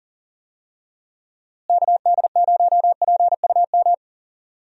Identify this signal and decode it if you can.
Morse code, heard as a keyed tone, about 37 wpm, 700 Hz. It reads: KD0PUM
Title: USA Call Signs - 6 Characters - Repeated 3x - 40wpm